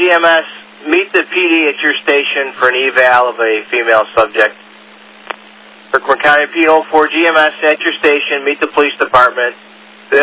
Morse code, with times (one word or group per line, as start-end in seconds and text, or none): none